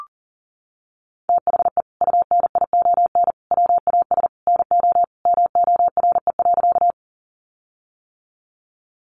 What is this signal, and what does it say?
Morse code, heard as a keyed tone, about 40 wpm, 700 Hz. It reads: THE UNION WAS NO MORE.
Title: Sentences from Top 1400 Words 40wpm